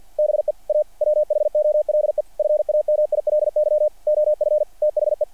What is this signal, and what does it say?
Morse code, heard as a keyed tone, about 48 wpm, 590 Hz. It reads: BE A WHOLE FAMILY OF THE
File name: archived / CRSnet